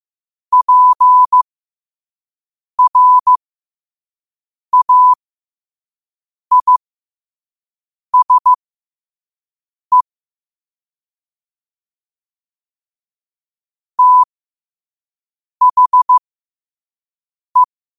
Morse code, heard as a keyed tone, about 15 wpm, spaced as for 5 wpm, 1000 Hz. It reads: PRAISE THE
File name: qotdfiles / 5wpm